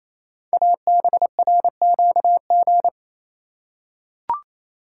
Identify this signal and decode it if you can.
Morse code, heard as a keyed tone, about 27 wpm, 700 Hz. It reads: ABRQG
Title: Code Groups - 5 Letters - 28wpm